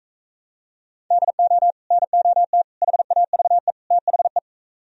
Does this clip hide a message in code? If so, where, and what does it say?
Yes: Morse code from 1.1 to 4.4 s, heard as a keyed tone, about 45 wpm, 700 Hz. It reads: DO NOT HAVE THE